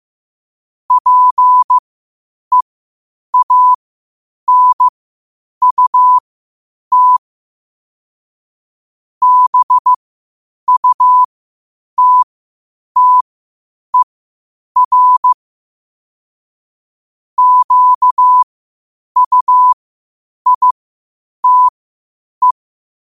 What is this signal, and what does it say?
Morse code, heard as a keyed tone, about 15 wpm, spaced as for 8 wpm, 1000 Hz. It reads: PEANUT BUTTER QUITE
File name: qotdfiles / 8wpm